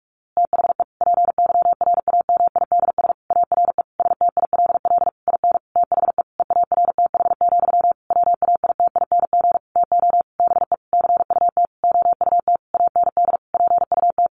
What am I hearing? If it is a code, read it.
Morse code, heard as a keyed tone, about 44 wpm, 700 Hz. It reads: THE PYRAMIDS ARE STILL IN THE EARTH, WAITING TO BE CUT OUT AND PUT
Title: A Sound Of Thunder 45wpm